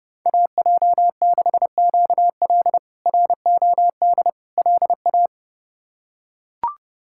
Morse code, heard as a keyed tone, about 30 wpm, 700 Hz. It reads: AJ6QL ROD LA
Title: K1USN SST QSO Practice - 30wpm